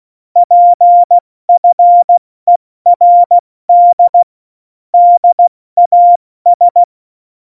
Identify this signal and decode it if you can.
Morse code, heard as a keyed tone, about 16 wpm, 700 Hz. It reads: PFERD DAS
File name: Morsekurs / Witze / Witz 27. November 2018, 70 F80